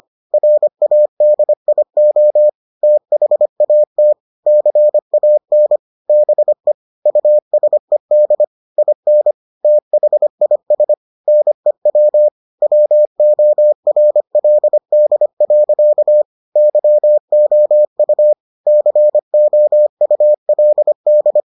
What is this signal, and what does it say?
Morse code, heard as a keyed tone, about 25 wpm, 600 Hz. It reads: RADIO THAT CAN BE USED IN THIS NEW WORLD. YOU COULD